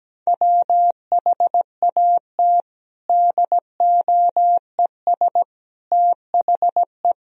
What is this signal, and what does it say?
Morse code, heard as a keyed tone, about 17 wpm, 700 Hz. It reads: WHAT DOES THE
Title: Sentences from Top 1000 Words - Review Part 3 - 17wpm